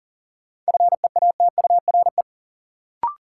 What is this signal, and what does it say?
Morse code, heard as a keyed tone, about 40 wpm, 700 Hz. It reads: FEATURE
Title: Sentences from Top 800 Words - New Word Drill - 40wpm